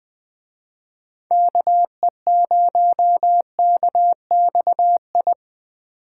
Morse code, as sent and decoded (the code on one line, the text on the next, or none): -.- . ----- -.- -..- ..
KE0KXI